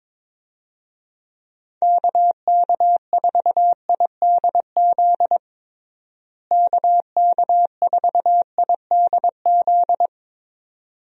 Morse code, heard as a keyed tone, about 22 wpm, 700 Hz. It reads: KK4IDZ KK4IDZ
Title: USA Call Signs - 6 Characters - Repeated 2x - 22wpm